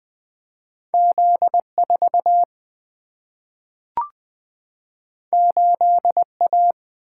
Morse code, heard as a keyed tone, about 20 wpm, 700 Hz. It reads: Z4 8A
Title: Single-Letter Single-Number 20wpm